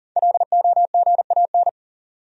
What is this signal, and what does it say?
Morse code, heard as a keyed tone, about 40 wpm, 700 Hz. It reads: LOGAN
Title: USA Common Names - 40wpm